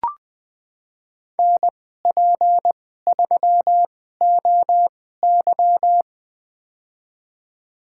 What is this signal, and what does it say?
Morse code, heard as a keyed tone, about 20 wpm, spaced as for 14 wpm, 700 Hz. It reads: NP3OY
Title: USA Call Signs - 2x Character Spacing - 20wpm